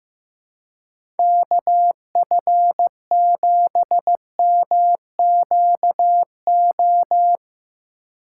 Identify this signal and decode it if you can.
Morse code, heard as a keyed tone, about 15 wpm, 700 Hz. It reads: KF7MQO
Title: USA Call Signs - 6 Characters - 4x Character Spacing - 15wpm